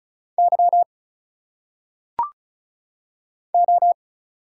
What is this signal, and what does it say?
Morse code, heard as a keyed tone, about 36 wpm, 700 Hz. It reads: Y O
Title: Single Letters - Phonetic Pronunciation 35wpm